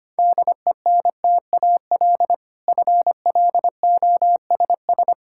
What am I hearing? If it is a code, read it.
Morse code, heard as a keyed tone, about 25 wpm, 700 Hz. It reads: DENTAL FLOSS
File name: practice / 25wpm / Difficult Words with Hints - 25wpm